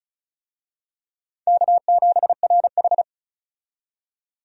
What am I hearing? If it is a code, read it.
Morse code, heard as a keyed tone, about 36 wpm, 700 Hz. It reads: K7RH
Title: USA Call Signs - 4 Characters - 35wpm